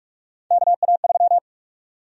Morse code, heard as a keyed tone, about 46 wpm, 700 Hz. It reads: KA3
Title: USA Call Sign Prefixes - 45wpm